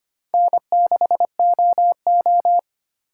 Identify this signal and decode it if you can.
Morse code, heard as a keyed tone, about 33 wpm, 700 Hz. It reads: N6OO